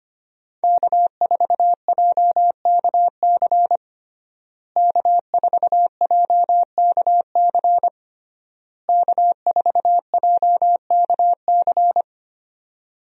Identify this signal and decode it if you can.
Morse code, heard as a keyed tone, about 25 wpm, 700 Hz. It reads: K4JKC K4JKC K4JKC
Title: USA Call Signs - 5 Characters - Repeated 3x - 25wpm